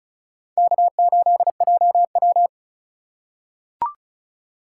Morse code, heard as a keyed tone, about 36 wpm, 700 Hz. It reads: K8JW
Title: USA Call Signs - 4 Characters - 4x Character Spacing - 35wpm